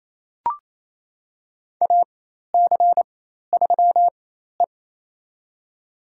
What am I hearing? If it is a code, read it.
Morse code, heard as a keyed tone, about 28 wpm, spaced as for 13 wpm, 700 Hz. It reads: AC3E
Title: USA Call Signs - 4 Characters - 4x Character Spacing - 28wpm